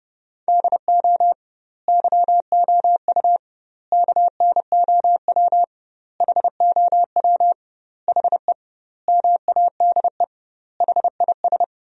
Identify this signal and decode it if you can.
Morse code, heard as a keyed tone, about 30 wpm, 700 Hz. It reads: DO YOU KNOW HOW HE MADE HIS
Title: 2x Word-Spacing - Sentences from Top 300 Words - 30wpm